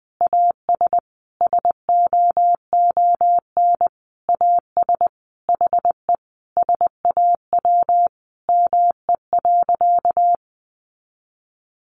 Morse code, heard as a keyed tone, about 20 wpm, 700 Hz. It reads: AS SOON AS HE SAW ME.
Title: Sentences from Top 1100 Words 20wpm